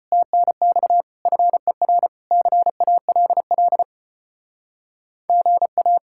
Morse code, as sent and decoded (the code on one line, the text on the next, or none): - -. -..- / ..-. .-. / -.-. .- .-.. .-.. / --. .-
TNX FER CALL GA